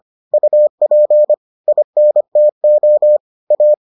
Morse code, heard as a keyed tone, about 25 wpm, 600 Hz. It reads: UP INTO A